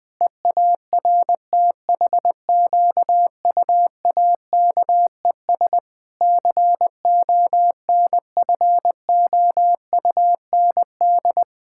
Morse code, heard as a keyed tone, about 20 wpm, 700 Hz. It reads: EARTHQUAKES CONFOUND
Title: Pangram Challenge v2 - 20wpm